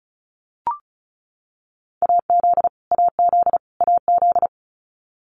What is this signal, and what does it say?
Morse code, heard as a keyed tone, about 34 wpm, 700 Hz. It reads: AZ AZ AZ
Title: US States Repeated 3x 35wpm